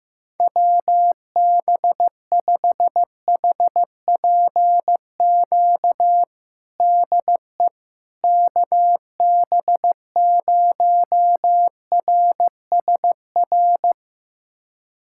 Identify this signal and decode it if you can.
Morse code, heard as a keyed tone, about 15 wpm, 700 Hz. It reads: WB5HPQ DE KB0RSR
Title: USA Callsign DE Callsign - 6 Characters - 15wpm